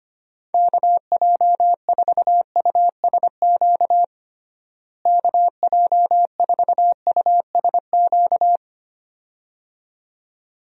Morse code, heard as a keyed tone, about 25 wpm, 700 Hz. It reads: KJ4USQ KJ4USQ
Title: USA Call Signs - 6 Characters - Repeated 2x - 25wpm